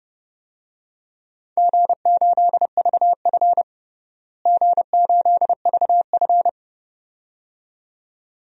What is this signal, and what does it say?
Morse code, heard as a keyed tone, about 32 wpm, 700 Hz. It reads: G8VF G8VF
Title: DXCC-100 Call Signs - 4 Characters - Repeated 2x - 30wpm